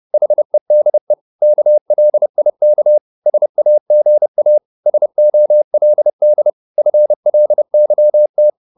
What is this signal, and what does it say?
Morse code, heard as a keyed tone, about 30 wpm, 600 Hz. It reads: HEDE KLIK SAGA SOLD FLYT
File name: Korte ord DK 30wpm